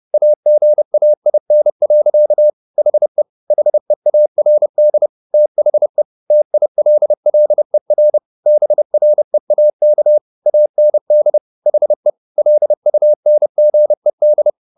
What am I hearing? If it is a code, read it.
Morse code, heard as a keyed tone, about 30 wpm, 600 Hz. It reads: AGAIN. HE HEARD THE TILLER BREAK AND HE LUNGED